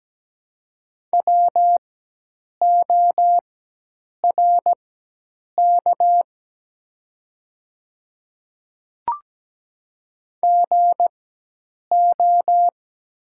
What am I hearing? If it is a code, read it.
Morse code, heard as a keyed tone, about 17 wpm, spaced as for 8 wpm, 700 Hz. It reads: WORK GO
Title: 4x Character Spacing - Top 200 Words - 17wpm